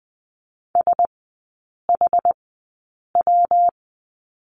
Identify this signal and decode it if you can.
Morse code, heard as a keyed tone, about 20 wpm, spaced as for 8 wpm, 700 Hz. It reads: SHW
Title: 138 Lesson 23e - Letter f - Three Characters 20wpm